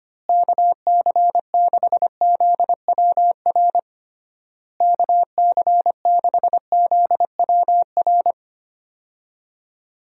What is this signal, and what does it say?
Morse code, heard as a keyed tone, about 25 wpm, 700 Hz. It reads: KC6ZWR KC6ZWR
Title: USA Call Signs - 6 Characters - Repeated 3x - 25wpm